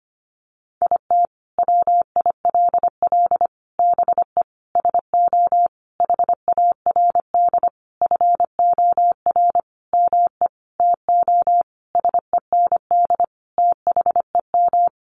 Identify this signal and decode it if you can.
Morse code, heard as a keyed tone, about 25 wpm, 700 Hz. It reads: IT WILL BE SO HARD FOR ME TO SEND THEM
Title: Sentences from Top 1500 Words 25wpm